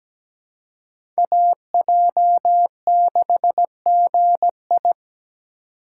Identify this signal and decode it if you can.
Morse code, heard as a keyed tone, about 17 wpm, 700 Hz. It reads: AJ6GI
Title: USA Call Signs - 5 Characters - 6x Character Spacing - 17wpm